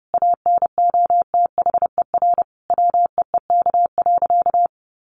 Morse code, heard as a keyed tone, about 30 wpm, 700 Hz. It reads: ANOTHER WEEK.
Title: Sentences from Top 1200 Words 30wpm